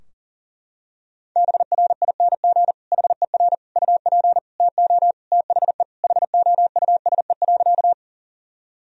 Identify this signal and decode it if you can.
Morse code, heard as a keyed tone, about 40 wpm, 700 Hz. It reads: BRING HER UP TO THE HOUSE.
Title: Sentences from Top 1000 Words - Review Part 3 - 40wpm